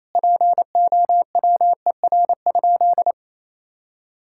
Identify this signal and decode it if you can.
Morse code, heard as a keyed tone, about 28 wpm, 700 Hz. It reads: POWER?
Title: Sentences from Top 1000 Words - Review Part 5 - 28wpm